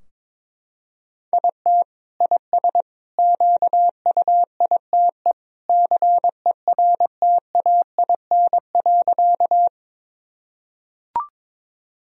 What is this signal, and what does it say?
Morse code, heard as a keyed tone, about 21 wpm, 700 Hz. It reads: IT IS QUITE CERTAIN.
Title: Sentences from Top 1000 Words - Review Part 2 - 22wpm